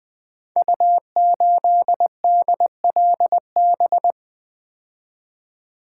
Morse code, heard as a keyed tone, about 20 wpm, 700 Hz. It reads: U8DLB
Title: Code Groups - 5 Letter:Numbers - 20wpm